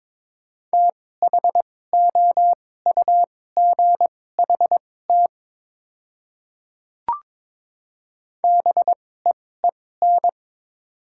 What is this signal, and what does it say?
Morse code, heard as a keyed tone, about 22 wpm, spaced as for 16 wpm, 700 Hz. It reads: THOUGHT BEEN
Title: 2x Character Spacing - Top 300 Words - 22wpm